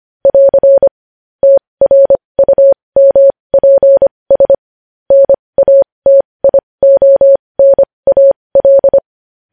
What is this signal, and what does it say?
Morse code, heard as a keyed tone, about 25 wpm, spaced as for 20 wpm, 550 Hz. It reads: <AR> TRUMPS NATIONAL
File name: podcast / News headlines-1x-25x20